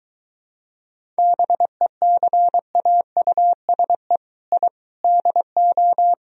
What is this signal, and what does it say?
Morse code, heard as a keyed tone, about 23 wpm, 700 Hz. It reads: BECAUSE I DO